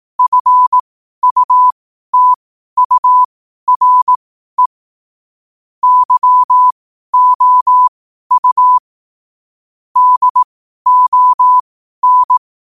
Morse code, heard as a keyed tone, about 18 wpm, spaced as for 12 wpm, 1000 Hz. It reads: FUTURE YOU DON
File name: qotdfiles / 12wpm